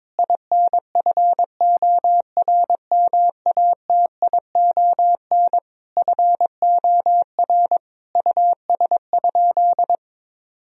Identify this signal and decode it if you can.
Morse code, heard as a keyed tone, about 22 wpm, 700 Hz. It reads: INFORMATION FOR US?
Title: Sentences from Top 1000 Words - Review Part 3 - 22wpm